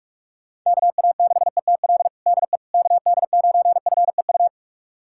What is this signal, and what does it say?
Morse code, heard as a keyed tone, about 41 wpm, 700 Hz. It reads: KA6ETL DE KD9FEU
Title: USA Callsign DE Callsign - 6 Characters - 45wpm